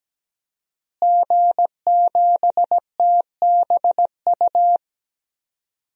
Morse code, heard as a keyed tone, about 17 wpm, 700 Hz. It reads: G7TBU
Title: DXCC-100 Call Signs - 5 Characters - Repeated 3x - 17wpm